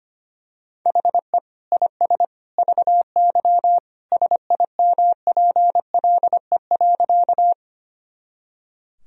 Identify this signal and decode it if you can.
Morse code, heard as a keyed tone, about 25 wpm, 700 Hz. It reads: HE IS VY SIMPLE.
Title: Sentences from Top 400 Words and Abbreviations 25wpm